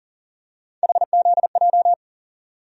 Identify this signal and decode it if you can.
Morse code, heard as a keyed tone, about 38 wpm, 700 Hz. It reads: HZJ
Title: USA Call Sign Postfixes - 3 Letters 40wpm